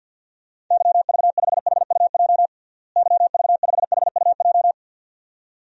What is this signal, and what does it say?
Morse code, heard as a keyed tone, about 52 wpm, 700 Hz. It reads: YV5HUJ YV5HUJ
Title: DXCC-100 Call Signs - 6 Characters - Repeated 2x - 50wpm